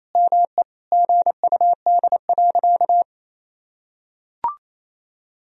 Morse code, heard as a keyed tone, about 28 wpm, 700 Hz. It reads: ME GUD.
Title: Sentences from Top 200 Words and Abbreviations 28wpm